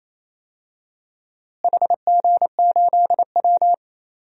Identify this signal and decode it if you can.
Morse code, heard as a keyed tone, about 28 wpm, 700 Hz. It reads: HG8W
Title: DXCC-100 Call Signs - 4 Characters - Repeated 3x - 28wpm